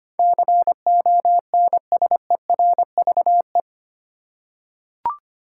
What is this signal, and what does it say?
Morse code, heard as a keyed tone, about 25 wpm, 700 Hz. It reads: CONSERVE
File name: 8 Letter Words - First 3 Letters Given - 25wpm